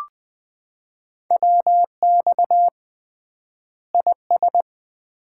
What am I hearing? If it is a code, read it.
Morse code, heard as a keyed tone, about 20 wpm, 700 Hz. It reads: WX IS
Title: Common QSO Phrases - 3x Word-Spacing - 20wpm